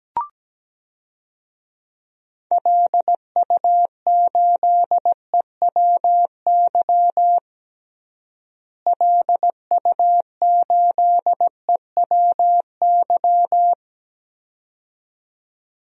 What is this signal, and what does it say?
Morse code, heard as a keyed tone, about 17 wpm, 700 Hz. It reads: LU8EWY LU8EWY
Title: DXCC-100 Call Signs - 6 Characters - Repeated 2x - 17wpm